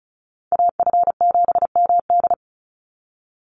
Morse code, heard as a keyed tone, about 34 wpm, 700 Hz. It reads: AF7MD
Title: NA SOTA Chaser Callsigns 35wpm